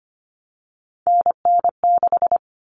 Morse code, heard as a keyed tone, about 25 wpm, 700 Hz. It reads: NN6
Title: All Cut Numbers - 3-Digit 25wpm